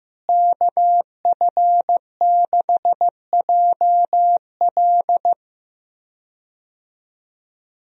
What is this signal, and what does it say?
Morse code, heard as a keyed tone, about 15 wpm, 700 Hz. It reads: KF6JL
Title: USA Call Signs - 5 Characters - Repeated 3x - 15wpm